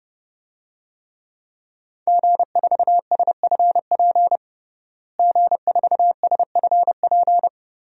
Morse code, heard as a keyed tone, about 32 wpm, 700 Hz. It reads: G4SFP G4SFP
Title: DXCC-100 Call Signs - 5 Characters - Repeated 2x - 30wpm